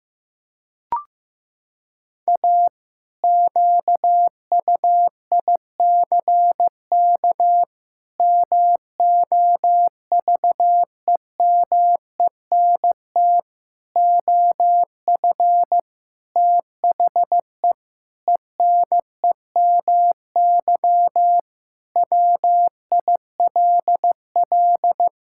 Morse code, heard as a keyed tone, about 15 wpm, 700 Hz. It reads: A QUICK MOVEMENT OF THE ENEMY WILL
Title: Pangram Challenge v2 - 15wpm